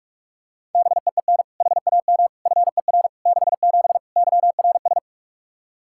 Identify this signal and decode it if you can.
Morse code, heard as a keyed tone, about 46 wpm, 700 Hz. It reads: BEEN HAM FER 67 YRS
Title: Practice QSO - Exchange 3 - More Info - 45wpm